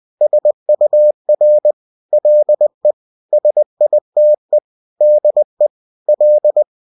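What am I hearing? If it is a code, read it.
Morse code, heard as a keyed tone, about 20 wpm, 600 Hz. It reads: SUR LE SITE DE L
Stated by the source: Bulletin sent in CW.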